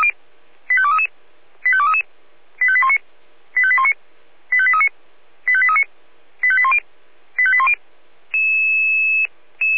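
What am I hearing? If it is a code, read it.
Five-tone selective call (ZVEI), heard as quick runs of five tones, tones of about 70 ms. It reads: 85233 85233 85819 85819 85839 85839 85810 85810
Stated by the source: ZVEI five-tone call 85230 received by SDR.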